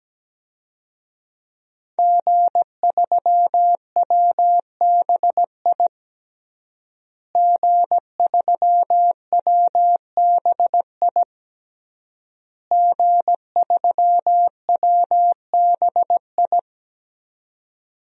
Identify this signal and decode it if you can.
Morse code, heard as a keyed tone, about 17 wpm, 700 Hz. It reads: G3WBI G3WBI G3WBI
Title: DXCC-100 Call Signs - 5 Characters - Repeated 3x - 17wpm